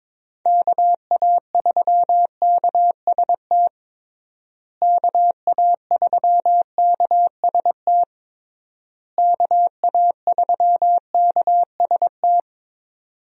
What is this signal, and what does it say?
Morse code, heard as a keyed tone, about 22 wpm, 700 Hz. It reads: KA3KST KA3KST KA3KST
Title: USA Call Signs - 6 Characters - Repeated 3x - 22wpm